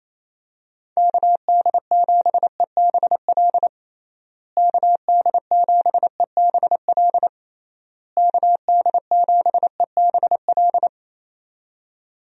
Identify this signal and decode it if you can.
Morse code, heard as a keyed tone, about 28 wpm, 700 Hz. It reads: KD7EBL KD7EBL KD7EBL
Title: USA Call Signs - 6 Characters - Repeated 3x - 28wpm